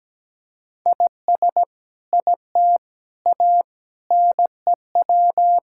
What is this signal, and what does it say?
Morse code, heard as a keyed tone, about 17 wpm, 700 Hz. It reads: IS IT A NEW
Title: Sentences from Top 1000 Words - Review Part 5 - 17wpm